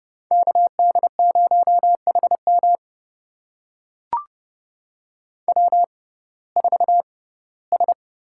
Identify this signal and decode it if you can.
Morse code, heard as a keyed tone, about 32 wpm, spaced as for 19 wpm, 700 Hz. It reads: KD0HM W4S
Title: USA Call Signs - 5 Characters - 6x Character Spacing - 30wpm